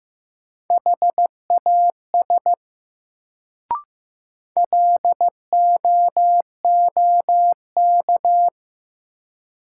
Morse code, heard as a keyed tone, about 16 wpm, 700 Hz. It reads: HAS LOOK
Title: Sets of 2 Words - New Word Drill - 15wpm